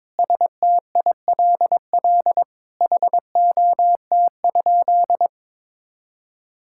Morse code, heard as a keyed tone, about 23 wpm, 700 Hz. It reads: STILL HOT?
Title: Sentences from Top 1000 Words - Review Part 6 - 22wpm